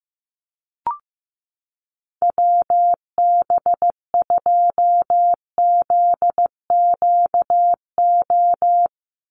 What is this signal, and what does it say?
Morse code, heard as a keyed tone, about 15 wpm, 700 Hz. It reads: WB2ZQO
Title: USA Call Signs 15wpm